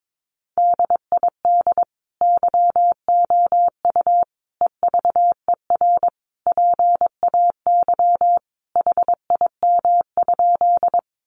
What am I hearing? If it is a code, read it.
Morse code, heard as a keyed tone, about 22 wpm, 700 Hz. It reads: DID YOU EVER PAY HIM?
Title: Sentences from Top 700 Words 22wpm